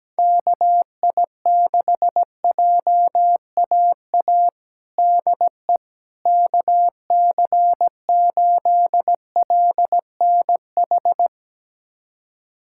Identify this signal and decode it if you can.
Morse code, heard as a keyed tone, about 17 wpm, 700 Hz. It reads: KI6JAA DE KC8LNH
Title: USA Callsign DE Callsign 17wpm